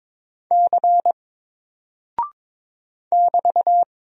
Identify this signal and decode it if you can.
Morse code, heard as a keyed tone, about 22 wpm, 700 Hz. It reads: C <BT>
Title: Single Character (Letter:Number:Punctuation:Prosign) Alt 22wpm